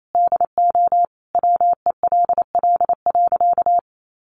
Morse code, heard as a keyed tone, about 28 wpm, 700 Hz. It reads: DO WELL.
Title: Sentences from Top 1400 Words 28wpm